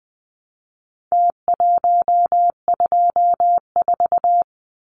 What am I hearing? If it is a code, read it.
Morse code, heard as a keyed tone, about 20 wpm, 700 Hz. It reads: T124
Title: Numbers - Contest Style - 4-Digit 20wpm